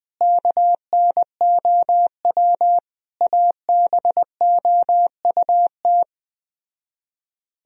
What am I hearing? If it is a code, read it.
Morse code, heard as a keyed tone, about 20 wpm, 700 Hz. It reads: KNOW ABOUT
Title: Speed-Racing - Sets of 5 Words 17wpm